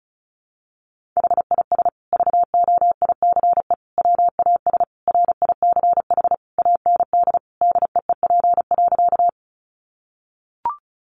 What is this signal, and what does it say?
Morse code, heard as a keyed tone, about 34 wpm, 700 Hz. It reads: HIS VOICE WAS RICH AND DEEP.
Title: Sentences from Top 1100 Words 35wpm